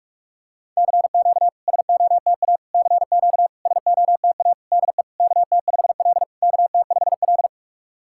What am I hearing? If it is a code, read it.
Morse code, heard as a keyed tone, about 44 wpm, 690 Hz. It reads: CQ SOTA CQ SOTA DE KT5L KT5L
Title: SOTA QSO Practice 45wpm